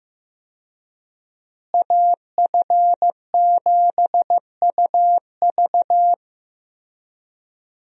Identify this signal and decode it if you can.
Morse code, heard as a keyed tone, about 15 wpm, 700 Hz. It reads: AF7UV
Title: USA Call Signs - 5 Characters - Repeated 2x - 15wpm